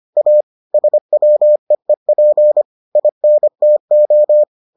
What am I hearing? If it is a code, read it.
Morse code, heard as a keyed tone, about 25 wpm, 600 Hz. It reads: A SWEEP INTO